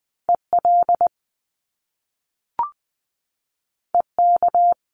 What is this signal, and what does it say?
Morse code, heard as a keyed tone, about 20 wpm, 700 Hz. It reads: EL EK
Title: Top Two Letter Combinations in Words 20wpm